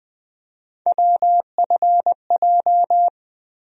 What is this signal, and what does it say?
Morse code, heard as a keyed tone, about 20 wpm, 700 Hz. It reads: WFJ
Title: USA Call Sign Postfixes - 3 Letters 20wpm